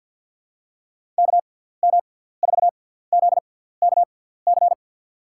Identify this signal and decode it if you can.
Morse code, heard as a keyed tone, about 49 wpm, spaced as for 17 wpm, 700 Hz. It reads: KM4ZKC
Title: USA Call Signs - 6 Characters - 6x Character Spacing - 50wpm